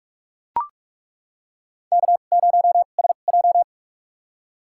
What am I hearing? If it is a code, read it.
Morse code, heard as a keyed tone, about 42 wpm, spaced as for 31 wpm, 700 Hz. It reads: K0SJ
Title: USA Call Signs - 4 Characters - 2x Character Spacing - 45wpm